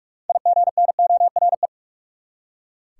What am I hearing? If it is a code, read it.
Morse code, heard as a keyed tone, about 42 wpm, 700 Hz. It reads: IGNORE
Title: Six Letter Words - 45wpm